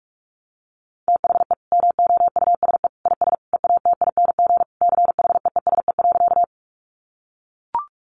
Morse code, heard as a keyed tone, about 44 wpm, 700 Hz. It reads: THE MOUSE IS EATING CHEESE.
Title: Easy Phrases 45wpm